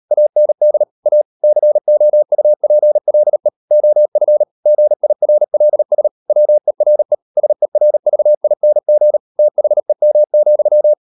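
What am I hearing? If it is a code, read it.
Morse code, heard as a keyed tone, about 38 wpm, 600 Hz. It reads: AND A COUPLE OF GIRLS WERE SERVING THEM,
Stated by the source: Morse code practice at 38 WPM, sending Ken Follett's novel Never in English, from CW med Gnister.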